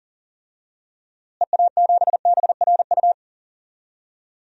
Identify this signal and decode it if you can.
Morse code, heard as a keyed tone, about 39 wpm, 700 Hz. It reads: EA7BRU
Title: DXCC-100 Call Signs - 6 Characters - Repeated 3x - 40wpm